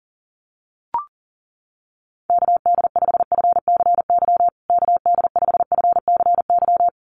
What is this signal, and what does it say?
Morse code, heard as a keyed tone, about 40 wpm, 700 Hz. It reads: KD5FCY KD5FCY
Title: USA Call Signs Repeated 2x 40wpm